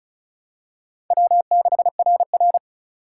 Morse code, heard as a keyed tone, about 34 wpm, 700 Hz. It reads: W6RR
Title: USA Call Signs - 4 Characters - 3x Character Spacing - 35wpm